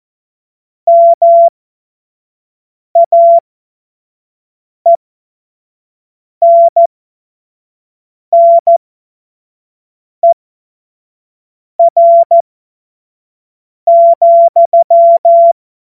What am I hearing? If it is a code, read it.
Morse code, heard as a keyed tone, about 14 wpm, spaced as for 5 wpm, 680 Hz. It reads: MAENNER,
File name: Morsekurs / Witze / Witz 09 April 2024 25 F70